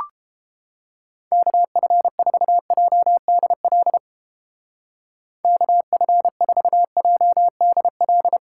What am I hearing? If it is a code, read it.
Morse code, heard as a keyed tone, about 32 wpm, 700 Hz. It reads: KF4JDL KF4JDL